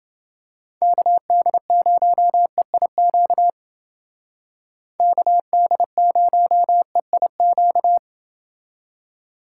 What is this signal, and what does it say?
Morse code, heard as a keyed tone, about 29 wpm, 700 Hz. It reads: KD0EIQ KD0EIQ